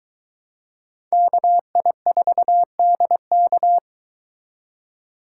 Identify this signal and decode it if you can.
Morse code, heard as a keyed tone, about 23 wpm, 700 Hz. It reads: KI4DK